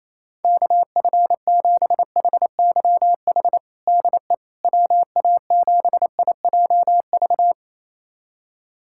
Morse code, heard as a keyed tone, about 28 wpm, 700 Hz. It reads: KF7HYH DE WA7IJV
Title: USA Callsign DE Callsign - 6 Characters - 28wpm